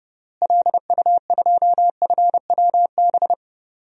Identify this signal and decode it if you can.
Morse code, heard as a keyed tone, about 30 wpm, 700 Hz. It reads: LU2FWB
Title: DXCC-100 Call Signs - 6 Characters - Repeated 2x - 30wpm